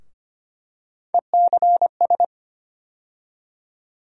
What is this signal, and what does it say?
Morse code, heard as a keyed tone, about 25 wpm, 700 Hz. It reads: ECS
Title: USA Call Sign Postfixes - 3 Letters 25wpm